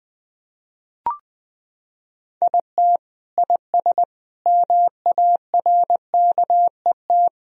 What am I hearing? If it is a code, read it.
Morse code, heard as a keyed tone, about 21 wpm, 700 Hz. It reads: IT IS MARKET
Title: Sentences from Top 1000 Words - Review Part 3 - 20wpm